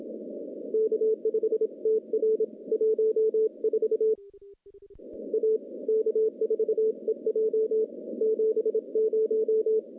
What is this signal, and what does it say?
Morse code, heard as a keyed tone, about 26 wpm, 440 Hz. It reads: K5TR 14 A K4EJ 70
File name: contests / CW-SS-k5tr- -0600